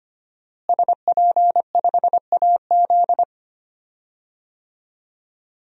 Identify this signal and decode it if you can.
Morse code, heard as a keyed tone, about 25 wpm, 700 Hz. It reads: SP5AZ
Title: DXCC-100 Call Signs - 5 Characters - Repeated 3x - 25wpm